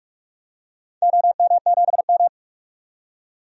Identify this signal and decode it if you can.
Morse code, heard as a keyed tone, about 42 wpm, 700 Hz. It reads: OM7M
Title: DXCC-100 Call Signs - 4 Characters - 45wpm